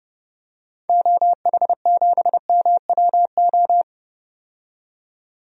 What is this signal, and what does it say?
Morse code, heard as a keyed tone, about 30 wpm, 700 Hz. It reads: OH7MWO
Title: DXCC-100 Call Signs - 6 Characters - 30wpm